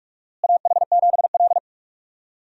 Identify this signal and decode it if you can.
Morse code, heard as a keyed tone, about 44 wpm, 700 Hz. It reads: AH7L
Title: USA Call Signs - 4 Characters - Repeated 3x - 45wpm